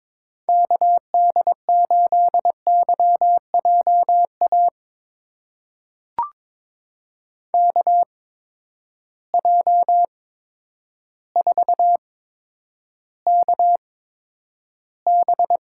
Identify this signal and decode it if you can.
Morse code, heard as a keyed tone, about 22 wpm, spaced as for 10 wpm, 700 Hz. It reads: KD8YJA KJ4KB
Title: USA Call Signs - 8x Character Spacing - 22wpm